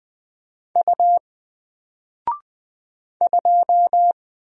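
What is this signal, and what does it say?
Morse code, heard as a keyed tone, about 20 wpm, 700 Hz. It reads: U 2
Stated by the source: Single letter and number Morse drill at 20 wpm, from N4LCD.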